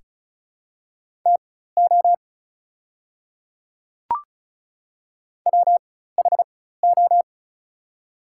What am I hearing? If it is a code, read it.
Morse code, heard as a keyed tone, about 36 wpm, spaced as for 16 wpm, 690 Hz. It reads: TO WHO
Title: 4x Character Spacing - Top 100 Words - 35wpm